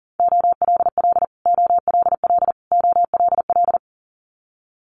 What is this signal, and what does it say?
Morse code, heard as a keyed tone, about 40 wpm, 700 Hz. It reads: OLL OLL OLL
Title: Multi-letter Phonemes Repeated 3x 40wpm